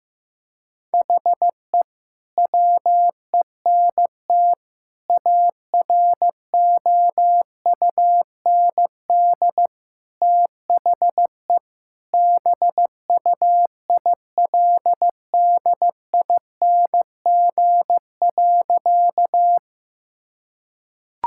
Morse code, heard as a keyed tone, about 15 wpm, 700 Hz. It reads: HE WENT AROUND THE BUILDING.
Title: Sentences from Top 1000 Words - Review Part 5 - 15wpm